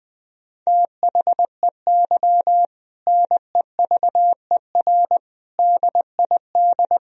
Morse code, heard as a keyed tone, about 20 wpm, 700 Hz. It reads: THEY NEVER DID
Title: Sentences from Top 300 Words and Abbreviations 20wpm